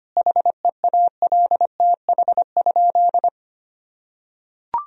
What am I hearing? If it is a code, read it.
Morse code, heard as a keyed tone, about 25 wpm, 700 Hz. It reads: HEALTH?
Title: Sentences from Top 1000 Words - Review Part 3 - 25wpm